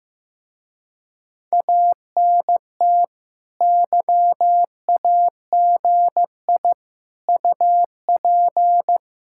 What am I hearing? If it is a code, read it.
Morse code, heard as a keyed tone, about 16 wpm, 700 Hz. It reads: ANT YAGI UP
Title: Speed-Racing Common QSO Phrases Encore - 15wpm